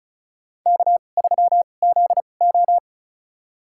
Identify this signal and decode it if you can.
Morse code, heard as a keyed tone, about 36 wpm, spaced as for 25 wpm, 690 Hz. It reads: K3ZO
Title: USA Call Signs - 4 Characters - 2x Character Spacing - 35wpm